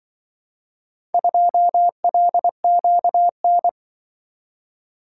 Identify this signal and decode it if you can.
Morse code, heard as a keyed tone, about 24 wpm, 700 Hz. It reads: 2LQN
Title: Speed-Racing - Four Letter-Number Combinations - 20wpm